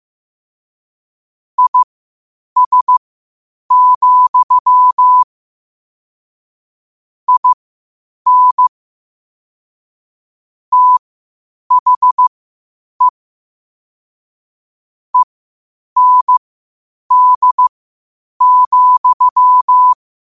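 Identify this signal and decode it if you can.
Morse code, heard as a keyed tone, about 15 wpm, spaced as for 8 wpm, 1000 Hz. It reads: IS, IN THE END,
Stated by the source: Morse practice at 8 WPM from Morse Resource.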